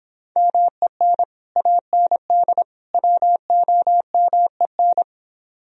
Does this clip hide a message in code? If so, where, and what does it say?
Yes: Morse code from 0.4 to 5.0 s, heard as a keyed tone, about 26 wpm, 700 Hz. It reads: MEN AND WOMEN